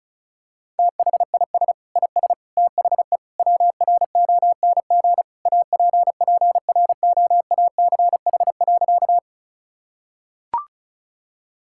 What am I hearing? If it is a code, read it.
Morse code, heard as a keyed tone, about 36 wpm, 700 Hz. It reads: THIS IS THE WRONG APPROACH.
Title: Sentences from Top 1000 Words - Review Part 6 - 35wpm